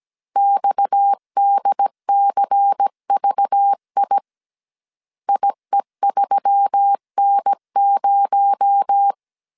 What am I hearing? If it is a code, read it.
Morse code, heard as a keyed tone, about 17 wpm, 800 Hz. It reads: XDCVI IE3N0
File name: CW / DET / 1 6 1500C S C 16wpm 2733